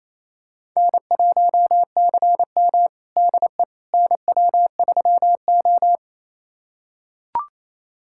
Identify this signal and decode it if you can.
Morse code, heard as a keyed tone, about 27 wpm, 700 Hz. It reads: N1CM DE NW3O
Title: USA Callsign DE Callsign - 4 Characters - 28wpm